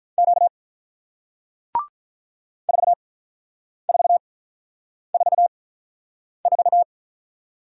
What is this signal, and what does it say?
Morse code, heard as a keyed tone, about 46 wpm, spaced as for 9 wpm, 700 Hz. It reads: Q 4444